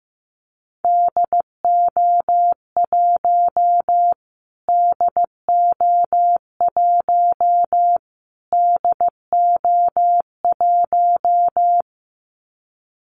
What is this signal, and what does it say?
Morse code, heard as a keyed tone, about 15 wpm, 700 Hz. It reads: DO1 DO1 DO1
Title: Callsign Prefixes for Top 25 DXCC - Repeated 3x - 15wpm